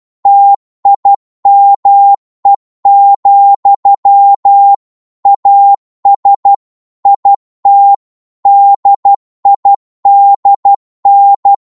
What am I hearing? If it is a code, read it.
Morse code, heard as a keyed tone, about 12 wpm, 800 Hz. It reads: TIME, AS IT DIDN
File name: Morse / psg-12wpm